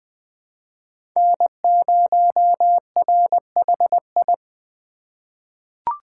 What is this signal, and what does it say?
Morse code, heard as a keyed tone, about 20 wpm, 700 Hz. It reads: N0RHI
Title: USA Call Signs - 5 Characters - 3x Character Spacing - 20wpm